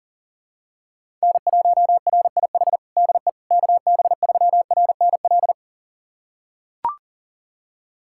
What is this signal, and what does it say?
Morse code, heard as a keyed tone, about 37 wpm, 700 Hz. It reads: N1RIH DE KB3RNL
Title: USA Callsign DE Callsign 40wpm